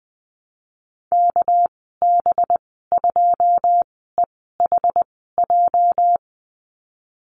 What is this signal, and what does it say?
Morse code, heard as a keyed tone, about 20 wpm, spaced as for 15 wpm, 700 Hz. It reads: KB2EHJ
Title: 219 Lesson 35g - Number 0 - Call Signs 20wpm